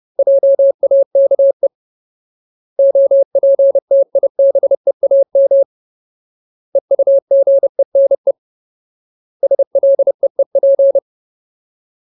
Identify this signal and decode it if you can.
Morse code, heard as a keyed tone, about 30 wpm, 550 Hz. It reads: JAKE OPTIBEAM EUGENE SLEEP
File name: Ultimate Word List / 30WPM Ultimate